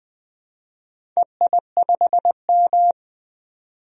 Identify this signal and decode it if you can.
Morse code, heard as a keyed tone, about 21 wpm, 700 Hz. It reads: EI5M
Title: DXCC-100 Call Signs - 4 Characters - 20wpm